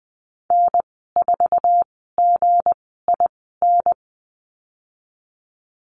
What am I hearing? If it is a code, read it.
Morse code, heard as a keyed tone, about 20 wpm, spaced as for 15 wpm, 700 Hz. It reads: N4GIN
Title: 173 Lesson 28g - Number 9 - Call Signs 20wpm